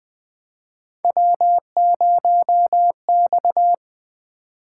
Morse code, heard as a keyed tone, about 20 wpm, 700 Hz. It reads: W0X
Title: Three Letter-Number Combinations - 20wpm